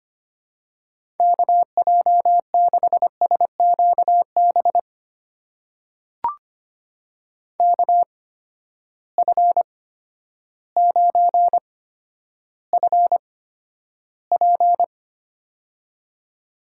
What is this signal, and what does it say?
Morse code, heard as a keyed tone, about 25 wpm, spaced as for 11 wpm, 700 Hz. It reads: KJ6SQB KF9FP
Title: USA Call Signs - 8x Character Spacing - 25wpm